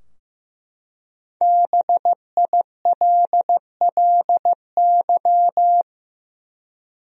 Morse code, heard as a keyed tone, about 15 wpm, 700 Hz. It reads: BILLY
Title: USA Common Names - Max 5 Letters - 15wpm